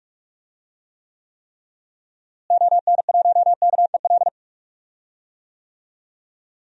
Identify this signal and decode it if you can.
Morse code, heard as a keyed tone, about 44 wpm, 700 Hz. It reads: ON1KEL
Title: DXCC-100 Call Signs - 6 Characters - Repeated 3x - 45wpm